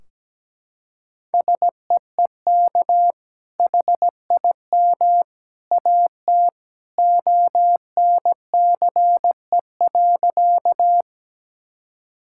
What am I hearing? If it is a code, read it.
Morse code, heard as a keyed tone, about 17 wpm, 700 Hz. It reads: SEEK HIM AT ONCE.
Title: Sentences from Top 1600 Words 17wpm